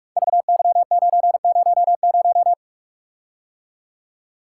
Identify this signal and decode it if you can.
Morse code, heard as a keyed tone, about 44 wpm, 690 Hz. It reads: VY900
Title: Code Groups - 5 Letter:Numbers - 45wpm